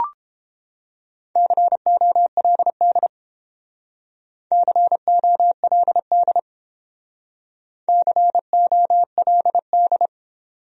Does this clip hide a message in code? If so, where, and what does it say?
Yes: Morse code from 1.4 to 10.1 s, heard as a keyed tone, about 30 wpm, 700 Hz. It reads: COLD COLD COLD